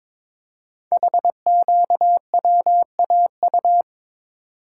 Morse code, heard as a keyed tone, about 23 wpm, 700 Hz. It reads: HQWAU
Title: Code Groups - 5 Letters - 22wpm